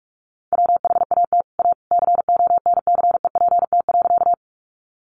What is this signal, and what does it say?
Morse code, heard as a keyed tone, about 44 wpm, 700 Hz. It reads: WHAT A CONCEPT.
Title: Sentences from Top 1100 Words 45wpm